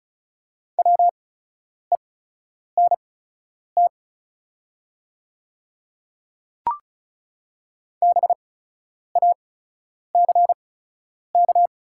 Morse code, heard as a keyed tone, about 34 wpm, spaced as for 10 wpm, 690 Hz. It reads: WENT BACK
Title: 8x Character Spacing - Top 200 Words - 35wpm